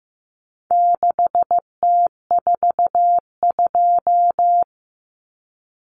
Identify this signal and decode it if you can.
Morse code, heard as a keyed tone, about 15 wpm, 700 Hz. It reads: 6T42
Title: Common Cut Numbers - 4-Digit 15wpm